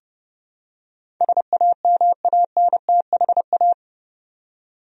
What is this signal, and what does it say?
Morse code, heard as a keyed tone, about 31 wpm, 700 Hz. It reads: SAMANTHA
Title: USA Common Names - 30wpm